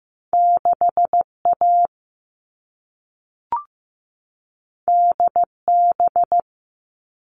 Morse code, heard as a keyed tone, about 15 wpm, 700 Hz. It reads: 6A DB
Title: All Cut Numbers - 2-Digit 15wpm